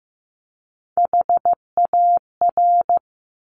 Morse code, heard as a keyed tone, about 15 wpm, 700 Hz. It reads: HAR
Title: Top Three Letter Combinations in Words 15wpm